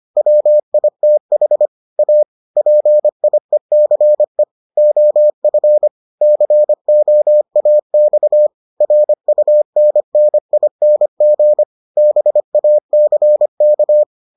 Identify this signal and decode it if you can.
Morse code, heard as a keyed tone, about 25 wpm, 600 Hz. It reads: WITH A PIECE OF COAX RUNNING BACK